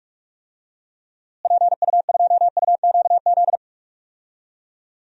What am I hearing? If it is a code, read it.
Morse code, heard as a keyed tone, about 46 wpm, 700 Hz. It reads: PU2UQZ
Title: DXCC-100 Call Signs - 6 Characters - Repeated 3x - 45wpm